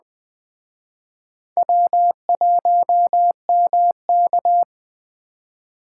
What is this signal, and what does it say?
Morse code, heard as a keyed tone, about 20 wpm, 700 Hz. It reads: W1MK
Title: Speed Racing - USA Call Signs 17wpm